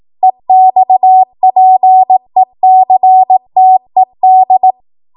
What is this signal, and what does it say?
Morse code, heard as a keyed tone, about 18 wpm, 750 Hz. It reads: EXPECTED
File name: Morse / 18WPM